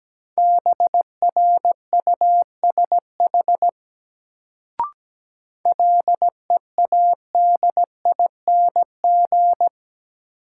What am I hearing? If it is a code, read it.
Morse code, heard as a keyed tone, about 17 wpm, 700 Hz. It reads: BRUSH LEADING
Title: Sentences from Top 1100 Words - New Word Drill Alt - 17wpm